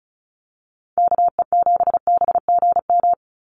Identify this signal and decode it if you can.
Morse code, heard as a keyed tone, about 34 wpm, 700 Hz. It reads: KE7BGM
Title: NA SOTA Activator Callsigns 35wpm